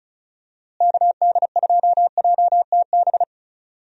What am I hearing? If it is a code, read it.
Morse code, heard as a keyed tone, about 36 wpm, 700 Hz. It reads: KD2JTB
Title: USA Call Signs - 6 Characters - Repeated 3x - 35wpm